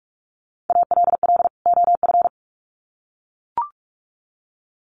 Morse code, heard as a keyed tone, about 44 wpm, 700 Hz. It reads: ALL OF
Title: Sets of 2 Words 45wpm